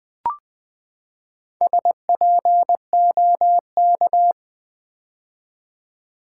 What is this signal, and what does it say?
Morse code, heard as a keyed tone, about 20 wpm, 700 Hz. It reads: SPOK
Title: Build A Word - 4 Letters At A Time - 12-Letter Words - 20wpm